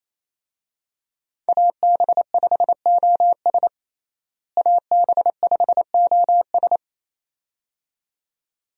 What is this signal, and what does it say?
Morse code, heard as a keyed tone, about 28 wpm, 700 Hz. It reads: AB5OS AB5OS
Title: USA Call Signs - 5 Characters - Repeated 2x - 28wpm